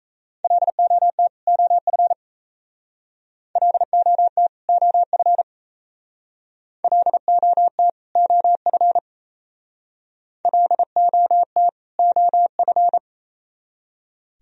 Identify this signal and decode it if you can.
Morse code, heard as a keyed tone, about 35 wpm, 700 Hz. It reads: LOT OF LOT OF LOT OF LOT OF
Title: Speed-Racing - Sets of 2 Words 28wpm